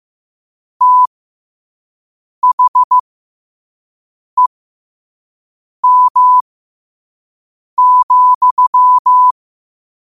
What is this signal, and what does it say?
Morse code, heard as a keyed tone, about 15 wpm, spaced as for 5 wpm, 1000 Hz. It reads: THEM,
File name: qotdfiles / 5wpm